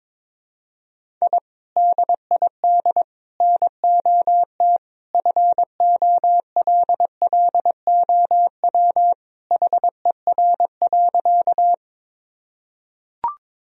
Morse code, heard as a keyed tone, about 21 wpm, 700 Hz. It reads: I DID NOT FOLLOW HER.
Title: Sentences from Top 1000 Words - Review Part 1 - 22wpm